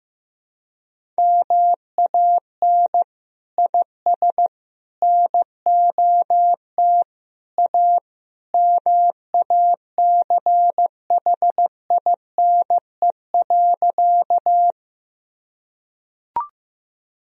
Morse code, heard as a keyed tone, about 15 wpm, 700 Hz. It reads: MAN IS NOT A MACHINE.
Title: Sentences from Top 1000 Words - Review Part 2 - 15wpm